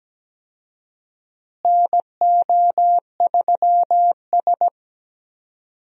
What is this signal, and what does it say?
Morse code, heard as a keyed tone, about 17 wpm, 700 Hz. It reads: NO3S
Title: USA Call Signs - 4 Characters - Repeated 3x - 17wpm